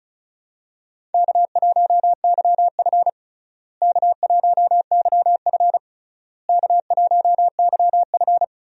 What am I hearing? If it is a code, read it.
Morse code, heard as a keyed tone, about 36 wpm, 700 Hz. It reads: K1YF K1YF K1YF
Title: USA Call Signs - 4 Characters - Repeated 3x - 35wpm